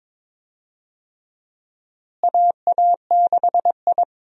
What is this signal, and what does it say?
Morse code, heard as a keyed tone, about 22 wpm, 700 Hz. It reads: AA6I
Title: USA Call Signs - 4 Characters - Repeated 3x - 22wpm